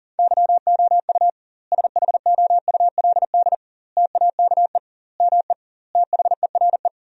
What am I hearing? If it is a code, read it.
Morse code, heard as a keyed tone, about 40 wpm, 700 Hz. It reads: YOU SHOULD TAKE ME THERE
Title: 2x Word-Spacing - Sentences from Top 300 Words - 40wpm